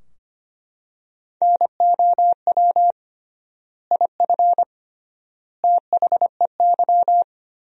Morse code, heard as a keyed tone, about 25 wpm, 700 Hz. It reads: NOW IF THEY
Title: 3x Word-Spacing - Sentences from Top 500 Words - 25wpm